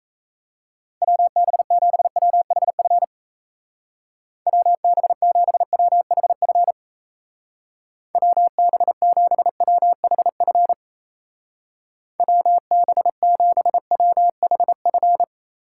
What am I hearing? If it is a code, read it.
Morse code, heard as a keyed tone, about 35 wpm, 700 Hz. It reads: WB7WHF WB7WHF WB7WHF WB7WHF
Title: Speed Racing - USA 2x3 Call Signs 28wpm